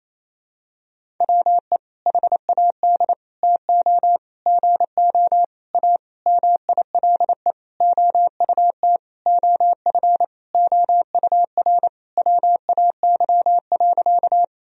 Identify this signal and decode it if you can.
Morse code, heard as a keyed tone, about 28 wpm, 700 Hz. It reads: WE HAD TO GO A MILE OUT OF OUR WAY.
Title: Sentences from Top 1000 Words - Review Part 1 - 28wpm